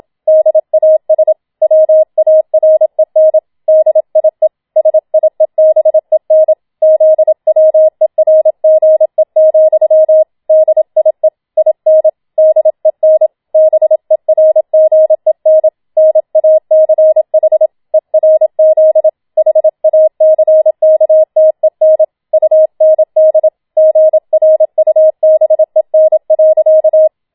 Morse code, heard as a keyed tone, about 27 wpm, 630 Hz. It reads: DAS WAREN DIE SIEBEN ZWERGE, DIE IN DEN BERGEN NACH ERZ HACKTEN UND GRUBEN.
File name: Schneewittchen Teil2 27wpm